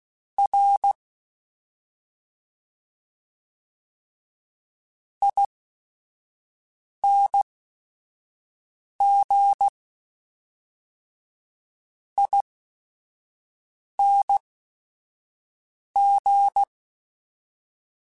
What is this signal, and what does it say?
Morse code, heard as a keyed tone, about 16 wpm, spaced as for 4 wpm, 800 Hz. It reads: RINGING
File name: Special INGs / Special INGs-mp3s / VerySlowToFaster Ing @5